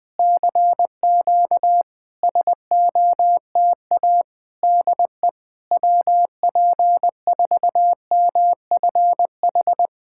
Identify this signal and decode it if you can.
Morse code, heard as a keyed tone, about 20 wpm, 700 Hz. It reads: CQ SOTA DE WP4MFH
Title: SOTA QSO Practice 20wpm